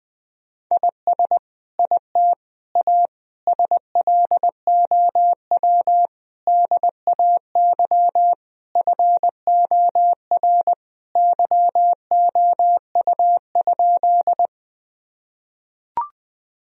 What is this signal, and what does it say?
Morse code, heard as a keyed tone, about 20 wpm, 700 Hz. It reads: IS IT A SLOW DAY FOR YOU?
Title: Sentences from Top 1000 Words - Review Part 2 - 20wpm